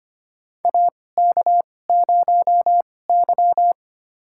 Morse code, heard as a keyed tone, about 25 wpm, spaced as for 18 wpm, 700 Hz. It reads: AK0Y
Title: USA Call Signs - 4 Characters - 2x Character Spacing - 25wpm